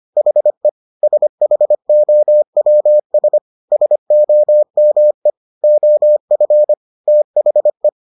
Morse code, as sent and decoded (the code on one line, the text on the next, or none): .... / ... .... --- .-- ... / ... --- -- . / --- ..-. / - .... .
HE SHOWS SOME OF THE